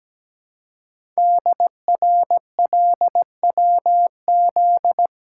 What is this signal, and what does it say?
Morse code, heard as a keyed tone, about 17 wpm, 700 Hz. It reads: DRLWZ
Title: Code Groups - 5 Letters - 17wpm